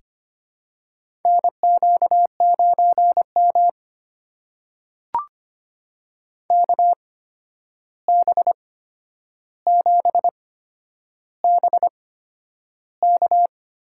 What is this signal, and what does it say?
Morse code, heard as a keyed tone, about 25 wpm, spaced as for 10 wpm, 700 Hz. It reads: NQ9M KB7BK
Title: USA Call Signs - 8x Character Spacing - 25wpm